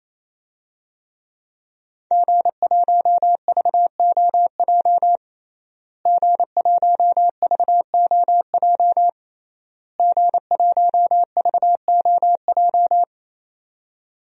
Morse code, heard as a keyed tone, about 28 wpm, 700 Hz. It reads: G1VOJ G1VOJ G1VOJ
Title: DXCC-100 Call Signs - 5 Characters - Repeated 3x - 28wpm